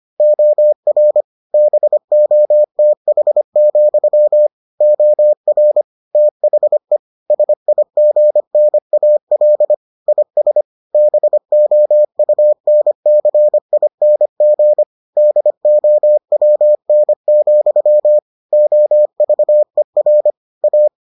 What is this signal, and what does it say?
Morse code, heard as a keyed tone, about 25 wpm, 600 Hz. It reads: OR BOTH, OR THE SIGNAL IS BOUNCING DOWN, OVER A